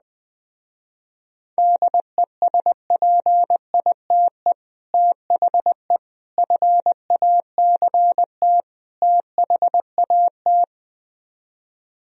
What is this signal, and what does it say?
Morse code, heard as a keyed tone, about 20 wpm, 700 Hz. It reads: DESPITE THE FACT THAT